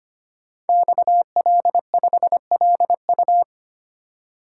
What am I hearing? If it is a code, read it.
Morse code, heard as a keyed tone, about 25 wpm, 700 Hz. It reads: XL5LU
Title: Code Groups - 5 Letter:Numbers - 25wpm